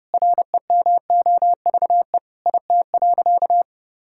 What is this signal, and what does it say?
Morse code, heard as a keyed tone, about 29 wpm, 700 Hz. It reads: REMOVE IT.
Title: Sentences from Top 1000 Words - Review Part 5 - 30wpm